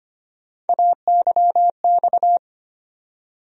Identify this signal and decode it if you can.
Morse code, heard as a keyed tone, about 25 wpm, 700 Hz. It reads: AYX
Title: USA Call Sign Postfixes - 3 Letters 25wpm